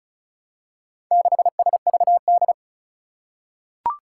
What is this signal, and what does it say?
Morse code, heard as a keyed tone, about 36 wpm, 700 Hz. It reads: 6SVD
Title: Four Challenging Letter-Number Combinations 35wpm